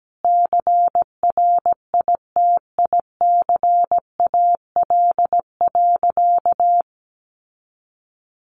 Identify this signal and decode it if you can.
Morse code, heard as a keyed tone, about 17 wpm, 700 Hz. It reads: CRITICAL.
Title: Sentences from Top 1100 Words 17wpm